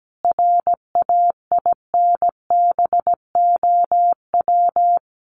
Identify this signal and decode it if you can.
Morse code, heard as a keyed tone, about 17 wpm, 700 Hz. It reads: RAINBOW
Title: Additional Easy Phrases 17wpm